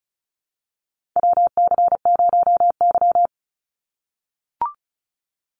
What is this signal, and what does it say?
Morse code, heard as a keyed tone, about 36 wpm, 700 Hz. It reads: WC0Y
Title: NA SOTA Chaser Callsigns 35wpm